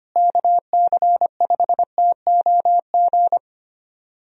Morse code, heard as a keyed tone, about 25 wpm, 700 Hz. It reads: KC5TOG
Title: USA Call Signs - 6 Characters - Repeated 2x - 25wpm